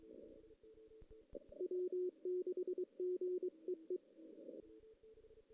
Morse code, heard as a keyed tone, about 23 wpm, 380 Hz. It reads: W6GEE
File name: 2016 JAN CW NAQP / 2016 Jan naqp k5ot - 0-2